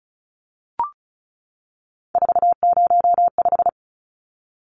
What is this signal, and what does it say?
Morse code, heard as a keyed tone, about 34 wpm, 700 Hz. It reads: 405
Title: Numbers - 3-Digit 35wpm